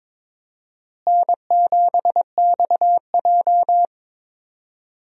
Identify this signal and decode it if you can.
Morse code, heard as a keyed tone, about 22 wpm, 700 Hz. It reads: N7XJ
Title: USA Call Signs - 4 Characters - Repeated 3x - 22wpm